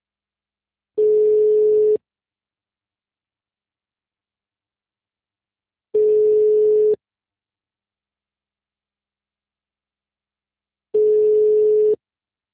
Ringback tone (Europe, 425 Hz).